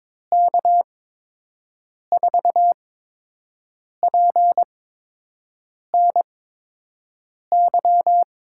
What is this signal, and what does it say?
Morse code, heard as a keyed tone, about 22 wpm, spaced as for 6 wpm, 700 Hz. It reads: K4PNY
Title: USA Call Signs - 6 Characters - 8x Character Spacing - 22wpm